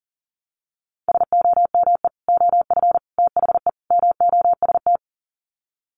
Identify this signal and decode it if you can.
Morse code, heard as a keyed tone, about 40 wpm, 700 Hz. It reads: SOME OF THE MOST
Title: Sets of 4 Words 40wpm